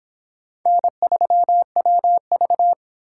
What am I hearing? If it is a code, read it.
Morse code, heard as a keyed tone, about 26 wpm, 700 Hz. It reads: N3WV